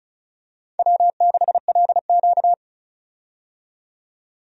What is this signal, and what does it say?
Morse code, heard as a keyed tone, about 34 wpm, 700 Hz. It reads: W6LQ
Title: USA Call Signs - 4 Characters - 35wpm